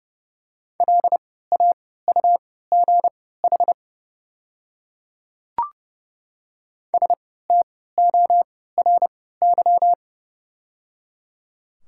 Morse code, heard as a keyed tone, about 30 wpm, spaced as for 17 wpm, 690 Hz. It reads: LAUGH STORY
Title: 3x Character Spacing - Top 500 Words - 30wpm